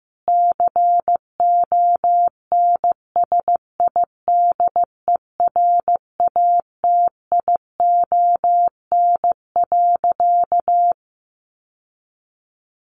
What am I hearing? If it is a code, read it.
Morse code, heard as a keyed tone, about 15 wpm, 700 Hz. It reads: CONSIDERATION.
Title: Sentences from Top 1500 Words 15wpm